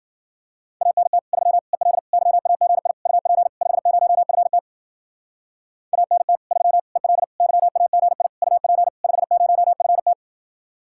Morse code, heard as a keyed tone, about 51 wpm, 700 Hz. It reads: ANT 3 EL YAGI UP 50FT ANT 3 EL YAGI UP 50FT